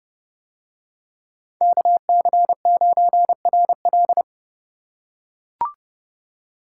Morse code, heard as a keyed tone, about 30 wpm, 700 Hz. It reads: KC9RL